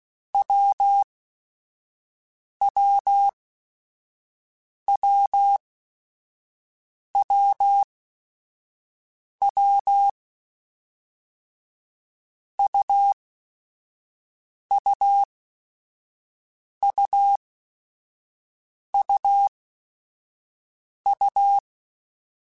Morse code, heard as a keyed tone, about 16 wpm, spaced as for 5 wpm, 800 Hz. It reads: WWWWW UUUUU
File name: Basic Sets / Basic Letter Groups Collective7 / AWUNDG@5